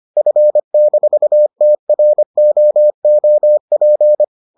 Morse code, heard as a keyed tone, about 25 wpm, 600 Hz. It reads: F-TROOP